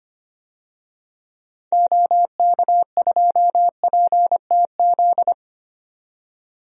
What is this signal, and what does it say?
Morse code, heard as a keyed tone, about 25 wpm, 690 Hz. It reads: OK2PTZ
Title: DXCC-100 Call Signs - 6 Characters - 25wpm